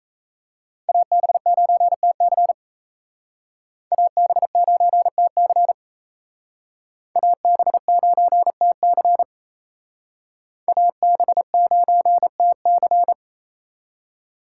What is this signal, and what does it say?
Morse code, heard as a keyed tone, about 34 wpm, 700 Hz. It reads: AB9TC AB9TC AB9TC AB9TC
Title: Speed Racing - USA 2x2 Call Signs 28wpm